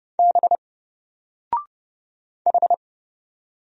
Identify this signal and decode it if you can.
Morse code, heard as a keyed tone, about 30 wpm, 700 Hz. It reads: B H
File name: Single-Letter-Number-30wpm